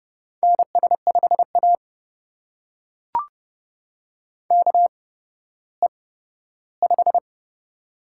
Morse code, heard as a keyed tone, about 29 wpm, spaced as for 14 wpm, 700 Hz. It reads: NS5A KE5
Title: USA Call Signs - 8x Character Spacing - 30wpm